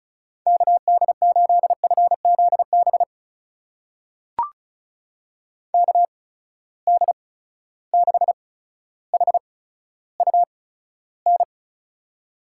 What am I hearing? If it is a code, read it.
Morse code, heard as a keyed tone, about 36 wpm, spaced as for 15 wpm, 700 Hz. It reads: KD8FZB KD6HUN
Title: USA Call Signs - 6 Characters - 8x Character Spacing - 35wpm